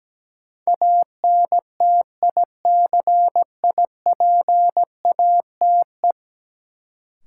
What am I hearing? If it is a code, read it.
Morse code, heard as a keyed tone, about 17 wpm, 700 Hz. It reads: ANTICIPATE
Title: Sentences from Top 1500 Words - New Word Drill - 17wpm